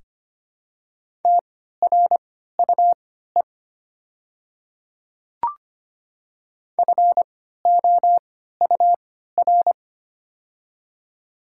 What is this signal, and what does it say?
Morse code, heard as a keyed tone, about 25 wpm, spaced as for 14 wpm, 700 Hz. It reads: TRUE FOUR
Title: 3x Character Spacing - Top 500 Words - 25wpm